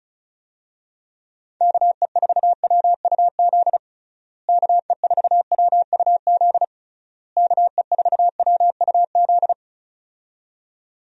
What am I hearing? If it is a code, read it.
Morse code, heard as a keyed tone, about 36 wpm, 700 Hz. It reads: KE4WUZ KE4WUZ KE4WUZ
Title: USA Call Signs - 6 Characters - Repeated 3x - 35wpm